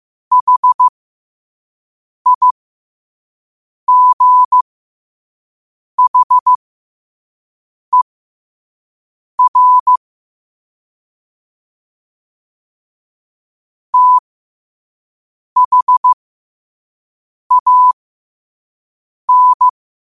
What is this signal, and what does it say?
Morse code, heard as a keyed tone, about 15 wpm, spaced as for 5 wpm, 1000 Hz. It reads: HIGHER THAN
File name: qotdfiles / 5wpm